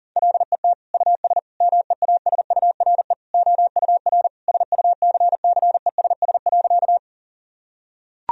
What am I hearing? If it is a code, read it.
Morse code, heard as a keyed tone, about 40 wpm, 700 Hz. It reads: LET US MEASURE OUR SUCCESS.
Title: Sentences from Top 1000 Words - Review Part 2 - 40wpm